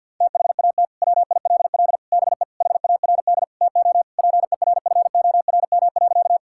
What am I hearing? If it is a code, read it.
Morse code, heard as a keyed tone, about 50 wpm, 700 Hz. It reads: THAT WILL BE HARD TO PERFORM.
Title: Sentences from Top 1000 Words - Review Part 5 - 50wpm